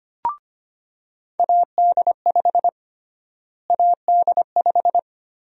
Morse code, heard as a keyed tone, about 25 wpm, 700 Hz. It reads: AD5 AD5
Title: USA Call Sign Prefixes - Repeated 2x - 25wpm